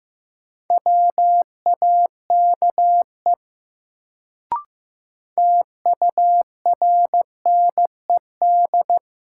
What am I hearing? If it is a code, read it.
Morse code, heard as a keyed tone, about 15 wpm, 700 Hz. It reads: WAKE TURNED
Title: Sets of 4 Words - New Word Drill - 15wpm